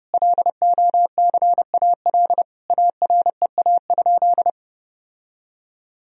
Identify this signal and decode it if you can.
Morse code, heard as a keyed tone, about 29 wpm, 700 Hz. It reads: LOCAL AREA?
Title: Sentences from Top 1000 Words - Review Part 3 - 30wpm